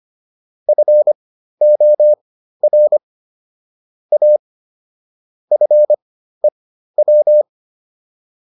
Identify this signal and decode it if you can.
Morse code, heard as a keyed tone, about 25 wpm, spaced as for 13 wpm, 600 Hz. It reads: FOR A FEW